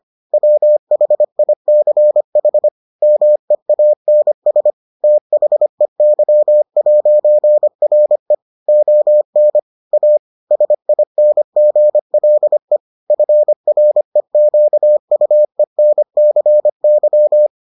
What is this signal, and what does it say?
Morse code, heard as a keyed tone, about 25 wpm, 600 Hz. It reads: WHICH MEANS THEY'RE ON A SINGLE FREQUENCY